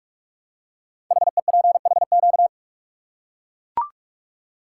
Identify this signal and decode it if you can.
Morse code, heard as a keyed tone, about 44 wpm, 700 Hz. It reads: HEPHQ
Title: Code Groups - 5 Letters - 45wpm